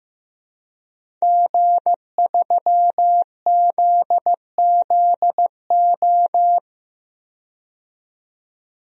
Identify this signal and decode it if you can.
Morse code, heard as a keyed tone, about 15 wpm, 700 Hz. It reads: G3ZZO
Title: DXCC-100 Call Signs - 5 Characters - 15wpm